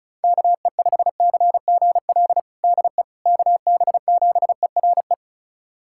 Morse code, heard as a keyed tone, about 36 wpm, 700 Hz. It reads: KE5CGL DE KB7ERE
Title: USA Callsign DE Callsign - 6 Characters - 35wpm